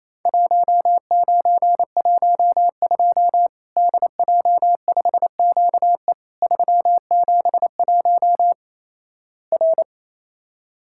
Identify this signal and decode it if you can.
Morse code, heard as a keyed tone, about 28 wpm, 700 Hz. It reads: 1912 DJ5QE 371 R
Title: Worked All Europe DX Contest - QTC Exchange - 28wpm